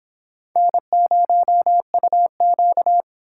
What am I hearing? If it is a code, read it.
Morse code, heard as a keyed tone, about 29 wpm, 700 Hz. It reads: N0UQ